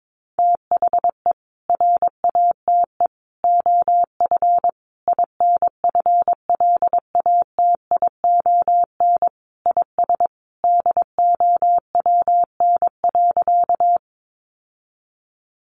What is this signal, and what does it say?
Morse code, heard as a keyed tone, about 22 wpm, 700 Hz. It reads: THE RATE OF INFLATION IS DOWN.
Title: Sentences from Top 1100 Words 22wpm